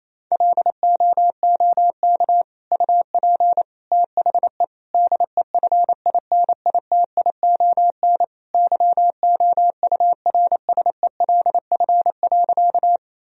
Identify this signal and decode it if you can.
Morse code, heard as a keyed tone, about 28 wpm, 700 Hz. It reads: LOOK UP THE DEFINITION YOURSELF.
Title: Sentences from Top 1000 Words - Review Part 4 - 28wpm